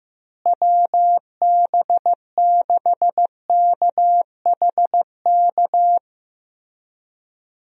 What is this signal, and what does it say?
Morse code, heard as a keyed tone, about 15 wpm, 700 Hz. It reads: WB6KHK
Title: USA Call Signs - 6 Characters - Repeated 3x - 15wpm